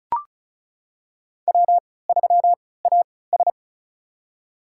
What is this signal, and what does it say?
Morse code, heard as a keyed tone, about 36 wpm, spaced as for 20 wpm, 700 Hz. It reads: W3AS
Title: USA Call Signs - 4 Characters - 3x Character Spacing - 35wpm